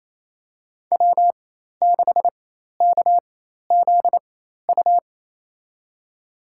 Morse code, heard as a keyed tone, about 28 wpm, spaced as for 13 wpm, 690 Hz. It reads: W6KZU
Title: USA Call Signs - 5 Characters - 4x Character Spacing - 28wpm